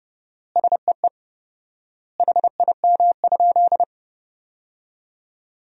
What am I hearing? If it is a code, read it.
Morse code, heard as a keyed tone, about 30 wpm, 700 Hz. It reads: SEE HIM?
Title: 4x Word-Spacing - Sentences from Top 100 Words - 30wpm